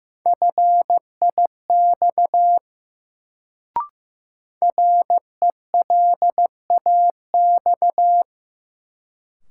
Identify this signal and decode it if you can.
Morse code, heard as a keyed tone, about 15 wpm, 700 Hz. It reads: FIX RELAX
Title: Sentences from Top 1400 Words - New Word Drill Alt - 15wpm